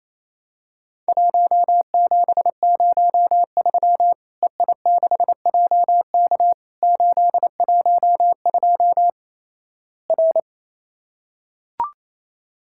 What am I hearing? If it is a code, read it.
Morse code, heard as a keyed tone, about 28 wpm, 700 Hz. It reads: 1703 EI6JK 812 R
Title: Worked All Europe DX Contest - QTC Exchange - 28wpm